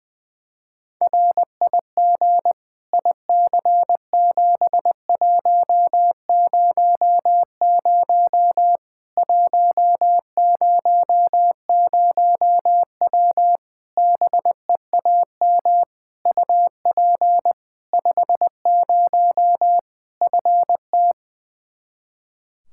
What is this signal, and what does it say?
Morse code, heard as a keyed tone, about 20 wpm, 700 Hz. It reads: RIG IC7100 100W BEAM UP 50 FT
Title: Practice QSO - Exchange 2 - Info - 20wpm